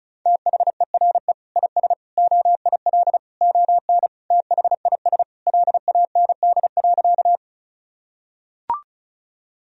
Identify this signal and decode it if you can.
Morse code, heard as a keyed tone, about 36 wpm, 700 Hz. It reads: THERE IS OIL ON THIS LAND.
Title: Sentences from Top 1000 Words - Review Part 4 - 35wpm